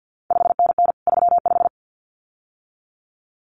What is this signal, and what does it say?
Morse code, heard as a keyed tone, about 50 wpm, 700 Hz. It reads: 5NN 35
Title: CQ WW DX Contest Exchange - 50wpm